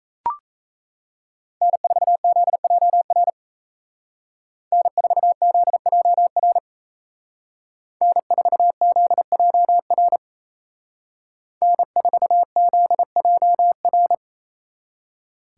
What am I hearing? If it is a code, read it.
Morse code, heard as a keyed tone, about 33 wpm, 700 Hz. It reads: N4ZJR N4ZJR N4ZJR N4ZJR
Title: Speed Racing - USA 1x3 Call Signs 28wpm